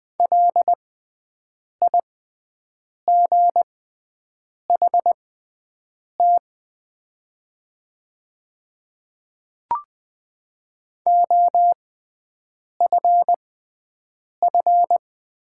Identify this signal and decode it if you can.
Morse code, heard as a keyed tone, about 20 wpm, spaced as for 7 wpm, 700 Hz. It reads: LIGHT OFF
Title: 6x Character Spacing - Top 300 Words - 20wpm